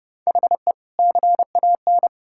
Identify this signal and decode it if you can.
Morse code, heard as a keyed tone, about 32 wpm, 700 Hz. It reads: HE CAN
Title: Sentences from Top 1600 Words 30wpm